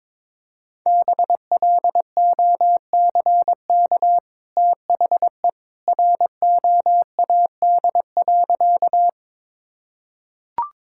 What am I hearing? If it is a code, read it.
Morse code, heard as a keyed tone, about 22 wpm, 700 Hz. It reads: BLOCK THE ROAD.
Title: Sentences from Top 1600 Words 22wpm